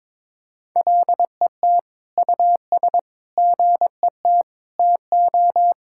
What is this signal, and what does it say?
Morse code, heard as a keyed tone, about 22 wpm, 700 Hz. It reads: LET US GET TO
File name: Sentences with 7-Word Phrases - 22wpm